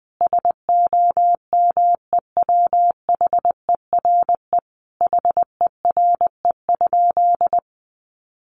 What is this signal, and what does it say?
Morse code, heard as a keyed tone, about 20 wpm, 700 Hz. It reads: SOMEWHERE HERE?
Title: Sentences from Top 1400 Words 20wpm